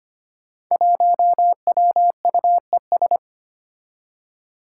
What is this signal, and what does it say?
Morse code, heard as a keyed tone, about 25 wpm, 700 Hz. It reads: 1WUES
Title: Five Challenging Letter-Number Combinations 25wpm